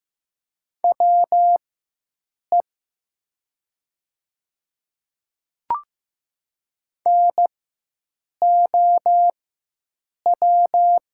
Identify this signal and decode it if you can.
Morse code, heard as a keyed tone, about 15 wpm, spaced as for 7 wpm, 700 Hz. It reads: WE NOW
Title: 4x Character Spacing - Top 100 Words - 15wpm